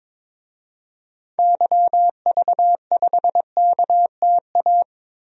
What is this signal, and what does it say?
Morse code, heard as a keyed tone, about 21 wpm, 700 Hz. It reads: YV5KTA
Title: DXCC-100 Call Signs - 6 Characters - 22wpm